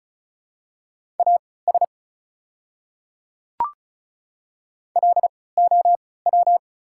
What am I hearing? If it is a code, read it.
Morse code, heard as a keyed tone, about 36 wpm, spaced as for 20 wpm, 700 Hz. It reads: AS LOW
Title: 3x Character Spacing - Top 200 Words - 35wpm